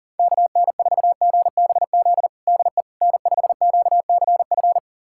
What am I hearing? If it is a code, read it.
Morse code, heard as a keyed tone, about 40 wpm, 700 Hz. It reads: KN4GBZ DE N5QCF
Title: USA Callsign DE Callsign 40wpm